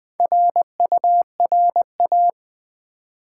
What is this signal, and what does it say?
Morse code, heard as a keyed tone, about 20 wpm, 700 Hz. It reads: RURA
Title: Build A Word - 1 Letter At A Time - 5-Letter Words - 20wpm